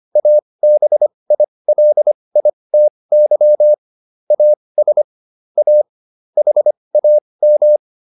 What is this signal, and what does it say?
Morse code, heard as a keyed tone, about 25 wpm, spaced as for 20 wpm, 600 Hz. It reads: ABILITY AS A HAM